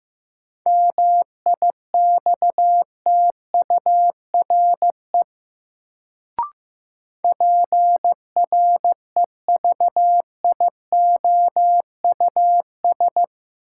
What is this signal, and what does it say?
Morse code, heard as a keyed tone, about 15 wpm, 700 Hz. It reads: MIXTURE PREVIOUS
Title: Sentences from Top 1500 Words - New Word Drill - 15wpm